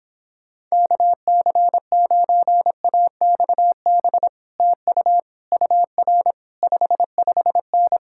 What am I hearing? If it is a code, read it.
Morse code, heard as a keyed tone, about 26 wpm, 700 Hz. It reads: KC9AXB TU UR 55N